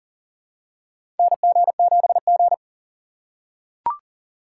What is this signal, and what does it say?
Morse code, heard as a keyed tone, about 39 wpm, 700 Hz. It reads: NG7G
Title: USA Call Signs - 2x Character Spacing - 40wpm